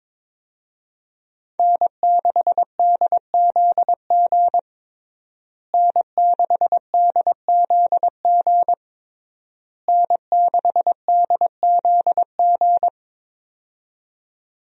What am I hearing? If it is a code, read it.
Morse code, heard as a keyed tone, about 22 wpm, 700 Hz. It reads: N6DZG N6DZG N6DZG
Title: USA Call Signs - 5 Characters - Repeated 3x - 22wpm